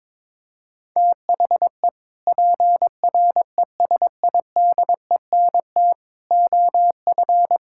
Morse code, heard as a keyed tone, about 22 wpm, 700 Hz. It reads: THE PRESIDENT OF
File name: practice / 22wpm / Sentences with 3-Word Phrases - 22wpm